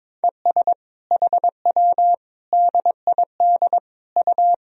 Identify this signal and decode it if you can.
Morse code, heard as a keyed tone, about 22 wpm, 700 Hz. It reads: ES HW DID U
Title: Sentences from Top 200 Words and Abbreviations 22wpm